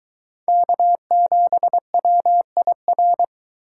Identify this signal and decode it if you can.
Morse code, heard as a keyed tone, about 22 wpm, 700 Hz. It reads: K7WIR